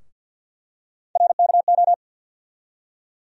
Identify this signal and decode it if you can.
Morse code, heard as a keyed tone, about 47 wpm, 700 Hz. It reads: RKO
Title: USA Call Sign Postfixes - 3 Letters 50wpm